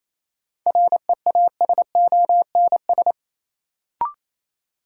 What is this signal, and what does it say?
Morse code, heard as a keyed tone, about 28 wpm, 700 Hz. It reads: REASONS
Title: Sets of 4 Words - New Word Drill - 28wpm